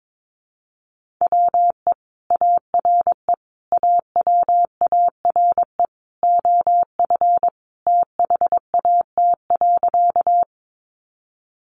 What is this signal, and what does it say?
Morse code, heard as a keyed tone, about 22 wpm, 700 Hz. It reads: WE ARE AWARE OF THAT.
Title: Sentences from Top 1200 Words 22wpm